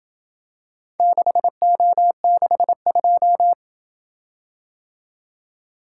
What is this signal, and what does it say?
Morse code, heard as a keyed tone, about 28 wpm, 700 Hz. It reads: 6O62